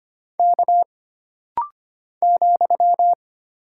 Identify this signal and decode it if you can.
Morse code, heard as a keyed tone, about 25 wpm, 700 Hz. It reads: K ,
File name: Single Character (Letter:Number:Punctuation:Prosign) - Rapid-Fire - 25wpm